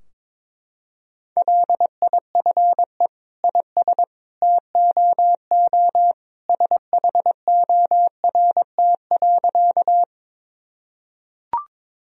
Morse code, heard as a keyed tone, about 22 wpm, 700 Hz. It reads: LIFE IS TOO SHORT.
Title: Sentences from Top 1000 Words - Review Part 2 - 22wpm